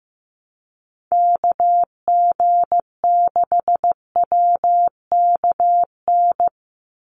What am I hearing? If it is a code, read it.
Morse code, heard as a keyed tone, about 15 wpm, 700 Hz. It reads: KG6WKN
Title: USA Call Signs 15wpm